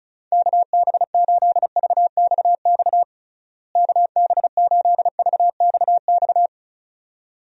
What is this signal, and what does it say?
Morse code, heard as a keyed tone, about 36 wpm, 700 Hz. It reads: KB8VXX KB8VXX
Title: USA Call Signs - 6 Characters - Repeated 2x - 35wpm